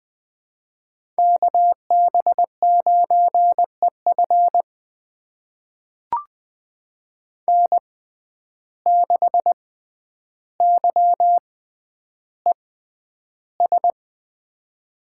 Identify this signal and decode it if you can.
Morse code, heard as a keyed tone, about 20 wpm, spaced as for 10 wpm, 700 Hz. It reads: KB9EF N6YES
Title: USA Call Signs - 5 Characters - 6x Character Spacing - 20wpm